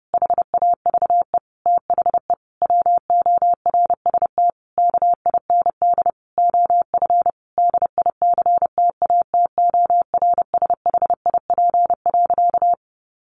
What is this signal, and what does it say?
Morse code, heard as a keyed tone, about 30 wpm, 700 Hz. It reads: HAVE THE WORST KIND OF DICTATORSHIP.
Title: A Sound Of Thunder 30wpm